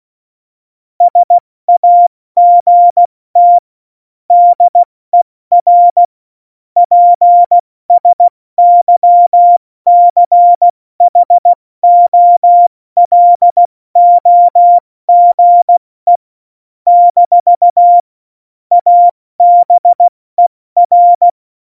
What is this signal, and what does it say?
Morse code, heard as a keyed tone, about 16 wpm, 700 Hz. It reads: SAGT DER PSYCHOLOGE - ABER
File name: Morsekurs / Witze / Witz 13. Februar 2018, 70 F80